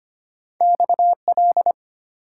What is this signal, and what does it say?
Morse code, heard as a keyed tone, about 25 wpm, 700 Hz. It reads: XL
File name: Two Letter-Number Combinations - 25wpm